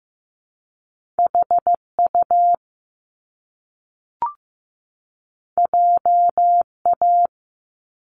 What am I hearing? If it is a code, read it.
Morse code, heard as a keyed tone, about 15 wpm, 700 Hz. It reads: HU JA
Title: Top Two Letter Combinations in Words 15wpm